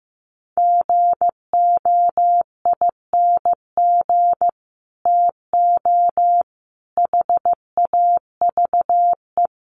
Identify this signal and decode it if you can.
Morse code, heard as a keyed tone, about 14 wpm, 700 Hz. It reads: GOING TO HAVE
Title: Sets of 5 Words 15wpm